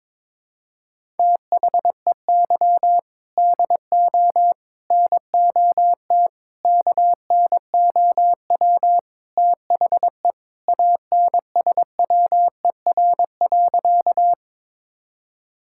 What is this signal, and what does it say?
Morse code, heard as a keyed tone, about 22 wpm, 700 Hz. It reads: THEY DO NOT KNOW THE ANSWER.
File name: practice / 22wpm / Sentences with 4-Word Phrases - 22wpm